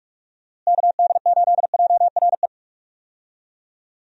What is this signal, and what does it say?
Morse code, heard as a keyed tone, about 46 wpm, 700 Hz. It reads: KD8JRE
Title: USA Call Signs - 6 Characters - Repeated 3x - 45wpm